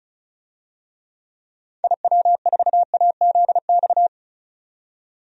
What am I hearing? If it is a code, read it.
Morse code, heard as a keyed tone, about 36 wpm, 700 Hz. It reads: IW4AZX
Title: DXCC-100 Call Signs - 6 Characters - 35wpm